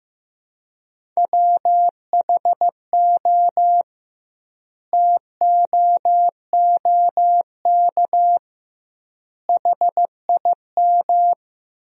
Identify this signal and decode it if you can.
Morse code, heard as a keyed tone, about 16 wpm, 700 Hz. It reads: WHO TOOK HIM
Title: 2x Word-Spacing - Sentences from Top 300 Words - 15wpm